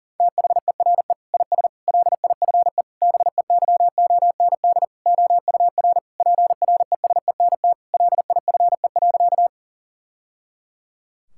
Morse code, heard as a keyed tone, about 37 wpm, 700 Hz. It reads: THERE IS LIFE BEYOND OUR PRESENT LIFE.
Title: Sentences from Top 1000 Words - Review Part 6 - 40wpm